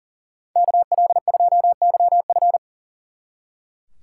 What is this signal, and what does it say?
Morse code, heard as a keyed tone, about 50 wpm, 700 Hz. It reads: KL2YF
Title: Speed Racing - USA 2x2 Call Signs 40wpm